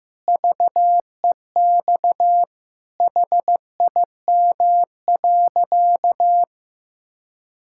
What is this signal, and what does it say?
Morse code, heard as a keyed tone, about 15 wpm, 700 Hz. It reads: VEX HIM.
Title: Pangram Challenge v2 - 15wpm